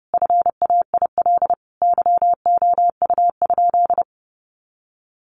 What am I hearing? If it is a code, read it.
Morse code, heard as a keyed tone, about 30 wpm, 700 Hz. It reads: FAIL YOU?
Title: Sentences from Top 1300 Words 30wpm